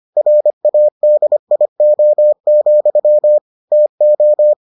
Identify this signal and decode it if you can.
Morse code, heard as a keyed tone, about 25 wpm, 600 Hz. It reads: RADIO, TO